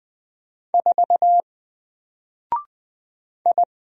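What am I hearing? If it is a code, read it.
Morse code, heard as a keyed tone, about 20 wpm, 700 Hz. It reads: 4 I
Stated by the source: Single letter and number Morse drill at 20 wpm, from N4LCD.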